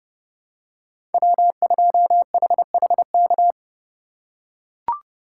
Morse code, heard as a keyed tone, about 30 wpm, 700 Hz. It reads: W2HHK
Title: USA Call Signs - 5 Characters - 3x Character Spacing - 30wpm